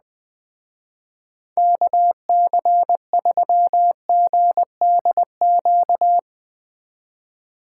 Morse code, heard as a keyed tone, about 20 wpm, 700 Hz. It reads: KC3GDQ